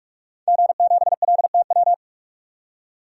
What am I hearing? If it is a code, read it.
Morse code, heard as a keyed tone, about 45 wpm, 700 Hz. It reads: G7LTW
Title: DXCC-100 Call Signs - 5 Characters - 45wpm